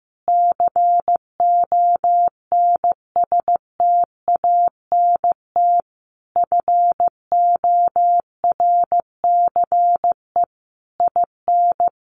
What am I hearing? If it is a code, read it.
Morse code, heard as a keyed tone, about 15 wpm, 700 Hz. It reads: CONSTANT FORCE IN
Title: Sentences from Top 1500 Words 15wpm